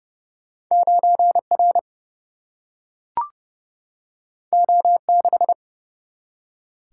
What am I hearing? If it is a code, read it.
Morse code, heard as a keyed tone, about 30 wpm, 700 Hz. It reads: 9 R O 6
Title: Single-Letter Single-Number 30wpm